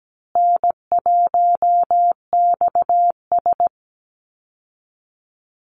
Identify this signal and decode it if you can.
Morse code, heard as a keyed tone, about 17 wpm, 700 Hz. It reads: N1XS
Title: Easy Call Signs Repeated 2x - 17wpm